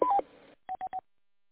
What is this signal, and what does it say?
There is no code.